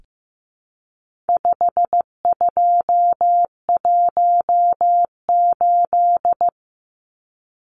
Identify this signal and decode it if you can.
Morse code, heard as a keyed tone, about 15 wpm, 700 Hz. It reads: 5218
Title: Numbers - 4-Digit 15wpm